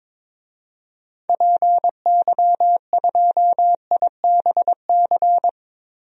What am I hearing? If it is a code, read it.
Morse code, heard as a keyed tone, about 22 wpm, 700 Hz. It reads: PY2IBC
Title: DXCC-100 Call Signs - 6 Characters - Repeated 3x - 22wpm